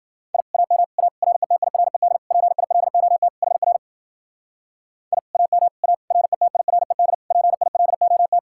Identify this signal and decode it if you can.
Morse code, heard as a keyed tone, about 48 wpm, 700 Hz. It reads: I AM A RETIRED PILOT HR I AM A RETIRED PILOT
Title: Speed-Racing Common QSO Phrases - 40wpm